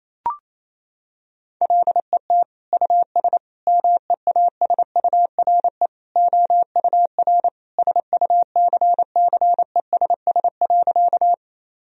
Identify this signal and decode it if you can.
Morse code, heard as a keyed tone, about 28 wpm, 700 Hz. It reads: LET US MEASURE OUR SUCCESS.
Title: Sentences from Top 1000 Words - Review Part 2 - 28wpm